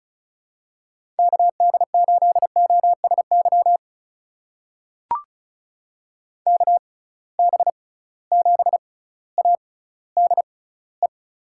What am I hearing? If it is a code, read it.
Morse code, heard as a keyed tone, about 34 wpm, spaced as for 18 wpm, 700 Hz. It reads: KD8OSY KB7ADE
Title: USA Call Signs - 6 Characters - 6x Character Spacing - 35wpm